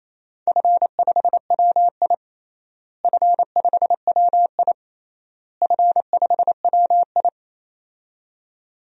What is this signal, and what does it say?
Morse code, heard as a keyed tone, about 27 wpm, 700 Hz. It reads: F5WI F5WI F5WI
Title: DXCC-100 Call Signs - 4 Characters - Repeated 3x - 28wpm